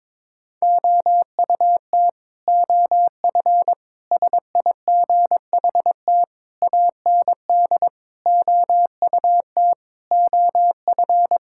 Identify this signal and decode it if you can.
Morse code, heard as a keyed tone, about 22 wpm, 700 Hz. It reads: OUT OF SIGHT AND OUT OF
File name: practice / 22wpm / Sets of 7 Words - 22wpm